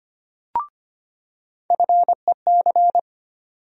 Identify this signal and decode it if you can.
Morse code, heard as a keyed tone, about 25 wpm, 700 Hz. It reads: FEC
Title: USA Call Sign Postfixes - 3 Letters 25wpm